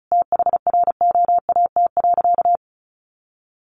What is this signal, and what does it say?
Morse code, heard as a keyed tone, about 34 wpm, 700 Hz. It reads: THROAT.
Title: A Sound Of Thunder 35wpm